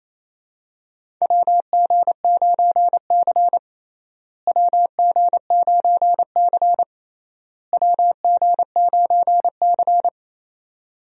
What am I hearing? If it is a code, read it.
Morse code, heard as a keyed tone, about 28 wpm, 700 Hz. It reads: WG9C WG9C WG9C
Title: USA Call Signs - 4 Characters - Repeated 3x - 28wpm